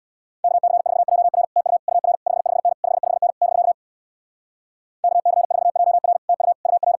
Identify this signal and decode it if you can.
Morse code, heard as a keyed tone, about 64 wpm, 700 Hz. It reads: KC4JR TU UR 55N 55N <BK> KC4JR TU UR